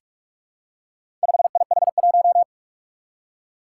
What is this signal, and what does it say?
Morse code, heard as a keyed tone, about 44 wpm, 700 Hz. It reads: 5IH1
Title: Four Challenging Letter-Number Combinations 45wpm